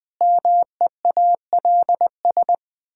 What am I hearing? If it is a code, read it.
Morse code, heard as a keyed tone, about 20 wpm, 700 Hz. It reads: MEALS
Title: Pangram Challenge v2 - 20wpm